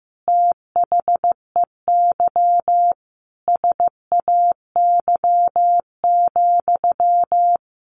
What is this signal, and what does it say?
Morse code, heard as a keyed tone, about 15 wpm, 700 Hz. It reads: THEY SAY,
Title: Sentences from Top 1100 Words 15wpm